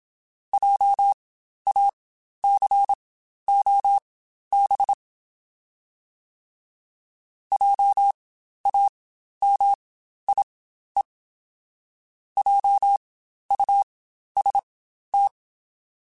Morse code, heard as a keyed tone, about 27 wpm, spaced as for 12 wpm, 800 Hz. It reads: JACOB JAMIE JUST
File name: Spaced Alphas / Spaced Alphas mp3s / Js sp @25